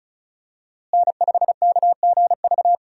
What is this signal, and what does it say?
Morse code, heard as a keyed tone, about 36 wpm, 700 Hz. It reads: N5KGV
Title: USA Call Signs - 5 Characters - Repeated 2x - 35wpm